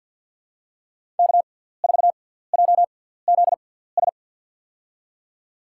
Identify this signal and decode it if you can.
Morse code, heard as a keyed tone, about 52 wpm, spaced as for 17 wpm, 700 Hz. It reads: K4JZS
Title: USA Call Signs - 5 Characters - 6x Character Spacing - 50wpm